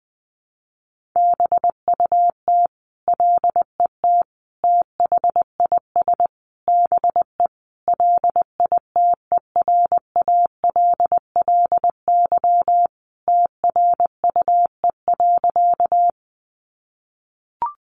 Morse code, heard as a keyed tone, about 20 wpm, 700 Hz. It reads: BUT LET THIS BE LITERALLY TRUE.
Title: Sentences from Top 1400 Words 20wpm